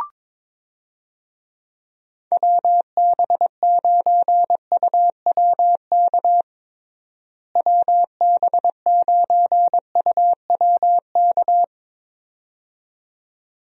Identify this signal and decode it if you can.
Morse code, heard as a keyed tone, about 22 wpm, 700 Hz. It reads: WB9UWK WB9UWK
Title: USA Call Signs - 6 Characters - Repeated 2x - 22wpm